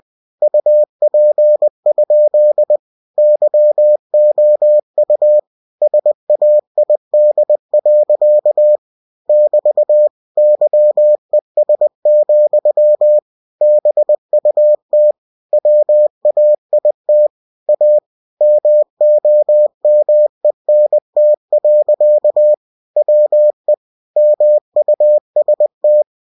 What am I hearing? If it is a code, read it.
Morse code, heard as a keyed tone, about 20 wpm, 600 Hz. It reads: UP? YOU SAID. <BT> YES, BUT WAIT A MOMENT. WE MUST